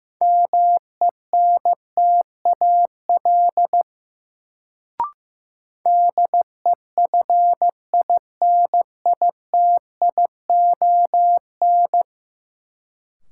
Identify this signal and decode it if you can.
Morse code, heard as a keyed tone, about 15 wpm, 700 Hz. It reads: MENTAL DEFINITION
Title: Sentences from Top 1000 Words - Review Part 4 - New Word Drill - 15wpm